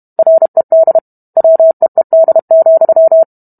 Morse code, heard as a keyed tone, about 32 wpm, 650 Hz. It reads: RED WEED,